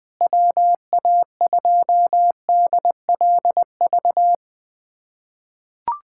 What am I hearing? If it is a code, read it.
Morse code, heard as a keyed tone, about 20 wpm, 700 Hz. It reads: WA2DLV
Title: USA Callsign DE Callsign - 6 Characters - 20wpm